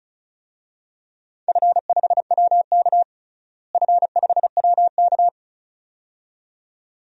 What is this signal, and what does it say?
Morse code, heard as a keyed tone, about 34 wpm, 700 Hz. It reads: F5WK F5WK
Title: DXCC-100 Call Signs - 4 Characters - Repeated 2x - 35wpm